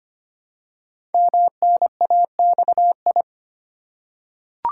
Morse code, heard as a keyed tone, about 25 wpm, 700 Hz. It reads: MNAXI
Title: Code Groups - 5 Letters - 25wpm